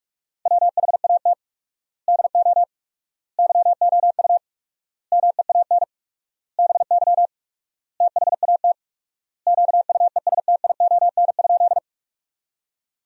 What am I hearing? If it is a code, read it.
Morse code, heard as a keyed tone, about 46 wpm, 700 Hz. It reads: WHAT DO YOU MEAN BY THAT QUESTION?
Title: 4x Word-Spacing - Sentences from Top 400 Words - 45wpm